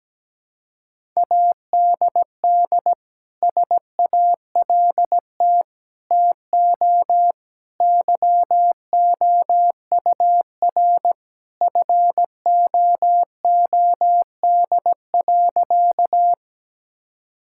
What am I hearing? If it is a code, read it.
Morse code, heard as a keyed tone, about 17 wpm, 700 Hz. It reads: ADD SALT TO YOUR FOOD.
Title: Sentences from Top 1000 Words - Review Part 5 - 17wpm